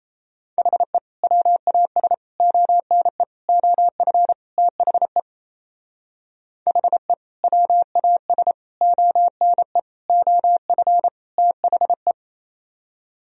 Morse code, heard as a keyed tone, about 36 wpm, 700 Hz. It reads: HE WAS ONE OF THE HE WAS ONE OF THE